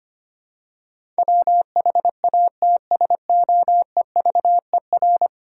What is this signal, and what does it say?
Morse code, heard as a keyed tone, about 25 wpm, 700 Hz. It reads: WHATSOEVER
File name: 10 Letter Words - First 3 Letters Given - 25wpm